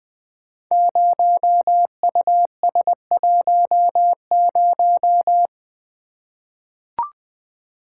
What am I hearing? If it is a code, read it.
Morse code, heard as a keyed tone, about 21 wpm, 700 Hz. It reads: 0US10
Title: Code Groups - 5 Letter:Numbers - 20wpm